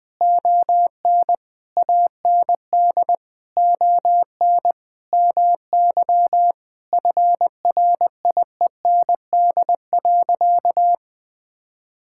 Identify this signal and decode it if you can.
Morse code, heard as a keyed tone, about 21 wpm, 700 Hz. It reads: ON AND ON MY FRIEND.
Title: Sentences with 6-Word Phrases - Spoken Beginning - 20wpm